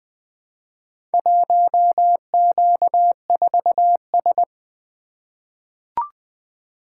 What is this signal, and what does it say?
Morse code, heard as a keyed tone, about 20 wpm, 700 Hz. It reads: 1Q4S
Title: Four Letter-Number Combinations - 20wpm